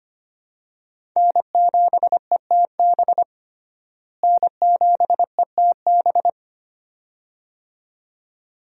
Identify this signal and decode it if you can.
Morse code, heard as a keyed tone, about 25 wpm, 700 Hz. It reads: N7ETB N7ETB
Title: USA Call Signs - 5 Characters - Repeated 2x - 25wpm